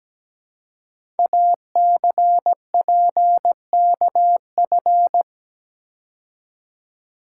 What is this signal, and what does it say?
Morse code, heard as a keyed tone, about 17 wpm, 700 Hz. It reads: ACPKF
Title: Code Groups - 5 Letter:Numbers - 17wpm